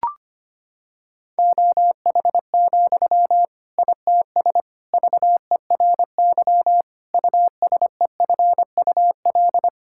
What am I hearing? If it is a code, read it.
Morse code, heard as a keyed tone, about 25 wpm, 700 Hz. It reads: OH, ITS VERY USEFUL
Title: Sentences from Top 1000 Words - Review Part 6 - 25wpm